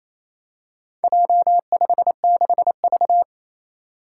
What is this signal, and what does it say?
Morse code, heard as a keyed tone, about 27 wpm, 700 Hz. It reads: J56V
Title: Four Challenging Letter-Number Combinations 28wpm